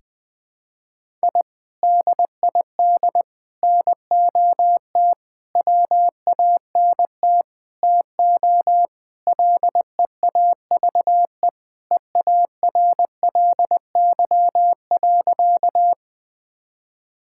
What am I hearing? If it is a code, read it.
Morse code, heard as a keyed tone, about 20 wpm, 700 Hz. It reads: I DID NOT WANT TO LEAVE EARLY.
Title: Sentences with 5-Word Phrases - 20wpm